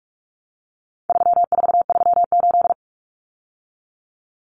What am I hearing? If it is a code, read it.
Morse code, heard as a keyed tone, about 44 wpm, 700 Hz. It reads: 3438
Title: Numbers - 4-Digit 45wpm